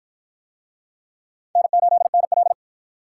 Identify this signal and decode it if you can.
Morse code, heard as a keyed tone, about 51 wpm, 700 Hz. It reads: N8NL